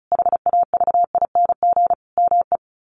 Morse code, heard as a keyed tone, about 34 wpm, 700 Hz. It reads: HAVING ME
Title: Sets of 5 Words Encore 35wpm